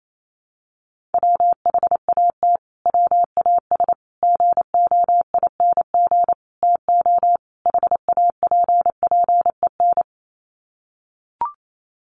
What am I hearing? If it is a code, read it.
Morse code, heard as a keyed tone, about 28 wpm, 700 Hz. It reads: WHAT WAS GOING TO HAPPEN
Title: Sets of 5 Words Encore 28wpm